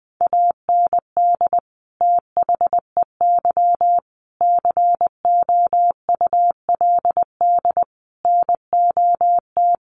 Morse code, heard as a keyed tone, about 20 wpm, 700 Hz. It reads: AND THEY COULD NOT
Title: Sentences from Top 1100 Words 20wpm